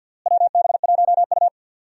Morse code, heard as a keyed tone, about 47 wpm, 700 Hz. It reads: WB1U
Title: USA Call Signs - 4 Characters - Repeated 3x - 50wpm